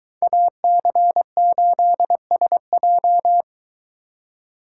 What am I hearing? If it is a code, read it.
Morse code, heard as a keyed tone, about 22 wpm, 700 Hz. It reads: AC8SJ